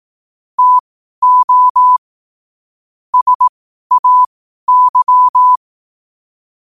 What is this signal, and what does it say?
Morse code, heard as a keyed tone, about 18 wpm, spaced as for 12 wpm, 1000 Hz. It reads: TO SAY
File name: qotdfiles / 12wpm